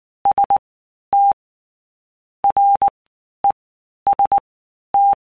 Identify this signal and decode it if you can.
Morse code, heard as a keyed tone, about 19 wpm, 800 Hz. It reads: S T R E S T
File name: Basic Letter Groups By Blocks / Groups@18 / AWUNDG@18